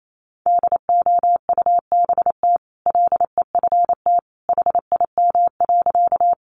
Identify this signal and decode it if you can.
Morse code, heard as a keyed tone, about 28 wpm, 700 Hz. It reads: DOUBT LEFT HIM.
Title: Sentences from Top 1500 Words 28wpm